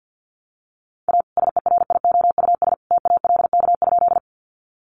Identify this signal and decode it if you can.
Morse code, heard as a keyed tone, about 48 wpm, 700 Hz. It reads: A SERIOUS TALK?
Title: Sentences from Top 1100 Words 50wpm